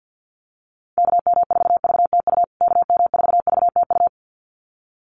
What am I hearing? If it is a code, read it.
Morse code, heard as a keyed tone, about 50 wpm, 700 Hz. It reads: KM4VTU KM4VTU
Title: USA Call Signs Repeated 2x 50wpm